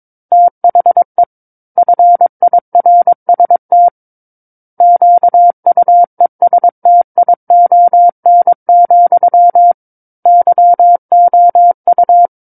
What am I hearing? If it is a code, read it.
Morse code, heard as a keyed tone, about 22 wpm, 700 Hz. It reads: THE FIRST QUESTION, YOU
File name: 06-Chapter-06 22 WPM